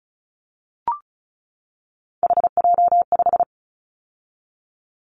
Morse code, heard as a keyed tone, about 34 wpm, 700 Hz. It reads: HJ5
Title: Callsign Prefixes for Top 100 DXCC - 35wpm